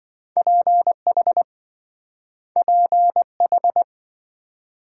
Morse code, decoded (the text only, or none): PH PH